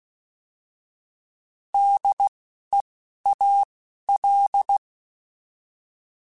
Morse code, heard as a keyed tone, about 16 wpm, spaced as for 12 wpm, 800 Hz. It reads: DEAL
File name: Alphabetical / Alphabetical mp3s / Ds @10